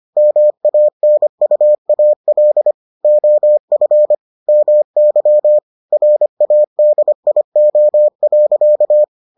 Morse code, heard as a keyed tone, about 25 wpm, 600 Hz. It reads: MANUAL OF MY RADIO.